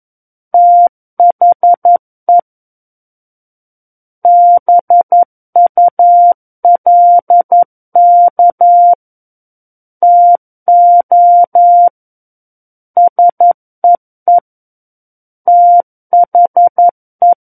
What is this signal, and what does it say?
Morse code, heard as a keyed tone, about 11 wpm, 700 Hz. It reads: THE BULK TO SEE THE
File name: 11 WPM